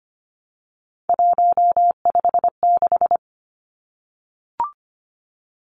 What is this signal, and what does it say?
Morse code, heard as a keyed tone, about 25 wpm, 700 Hz. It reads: 156
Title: Numbers - 3-Digit 25wpm